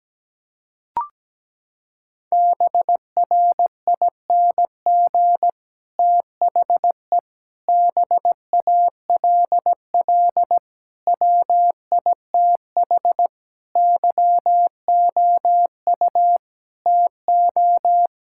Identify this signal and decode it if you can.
Morse code, heard as a keyed tone, about 17 wpm, 700 Hz. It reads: BRING THE BALL WITH YOU TO
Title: Sentences from Top 1000 Words - Review Part 6 - 17wpm